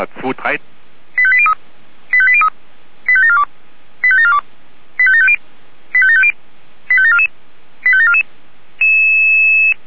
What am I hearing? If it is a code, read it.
Five-tone selective call (ZVEI), heard as quick runs of five tones, tones of about 70 ms, with speech. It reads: 85803 85803 85742 85742 85750 85750 85744 85744